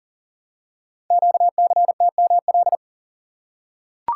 Morse code, heard as a keyed tone, about 42 wpm, 700 Hz. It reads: QCTML
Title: Code Groups - 5 Letters - 40wpm